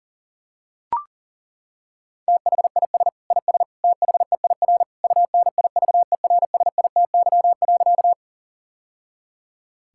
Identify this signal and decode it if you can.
Morse code, heard as a keyed tone, about 40 wpm, 700 Hz. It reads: THIS IS THEIR UNIVERSITY.
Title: Sentences from Top 1000 Words - Review Part 5 - 40wpm